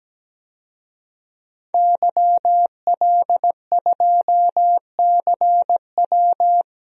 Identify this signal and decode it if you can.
Morse code, heard as a keyed tone, about 17 wpm, 700 Hz. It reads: YL2CW
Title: DXCC-100 Call Signs - 5 Characters - 17wpm